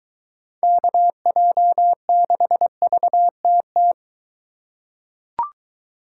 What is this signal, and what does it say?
Morse code, heard as a keyed tone, about 23 wpm, 700 Hz. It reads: KJ6VTT